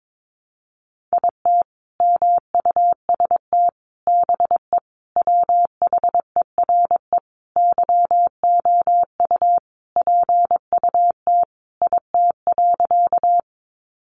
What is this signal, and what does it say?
Morse code, heard as a keyed tone, about 22 wpm, 700 Hz. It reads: IT MUST BE WHERE YOU PUT IT.
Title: Sentences from Top 200 Words 22wpm